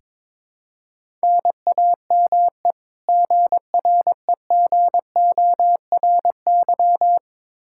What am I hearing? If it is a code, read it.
Morse code, heard as a keyed tone, about 21 wpm, 700 Hz. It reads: NAME GREGORY
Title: Practice QSO - Exchange 1 - Beginning - 22wpm